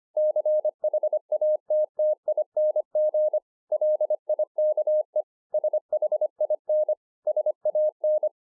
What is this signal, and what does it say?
Morse code, heard as a keyed tone, about 24 wpm, 610 Hz. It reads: CHATTING LIKE SHIN SAN